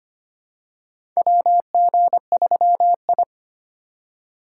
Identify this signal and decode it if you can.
Morse code, heard as a keyed tone, about 25 wpm, 700 Hz. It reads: WG3I
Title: USA Call Signs - 4 Characters - 3x Character Spacing - 25wpm